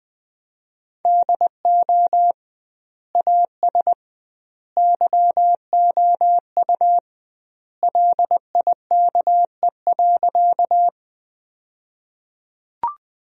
Morse code, heard as a keyed tone, about 21 wpm, 700 Hz. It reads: DO AS YOU LIKE.
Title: 2x Word-Spacing - Sentences from Top 100 Words - 20wpm